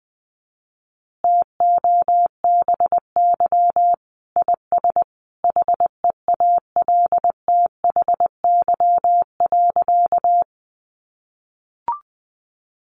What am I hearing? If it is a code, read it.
Morse code, heard as a keyed tone, about 20 wpm, 700 Hz. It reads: TOBY IS HEALTHY.
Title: Easy Phrases 20wpm